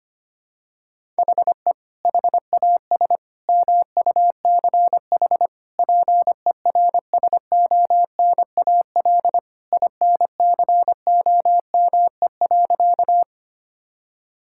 Morse code, heard as a keyed tone, about 25 wpm, 700 Hz. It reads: HE HAS MUCH PERSONAL INCOME.
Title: Sentences from Top 1000 Words - Review Part 5 - 25wpm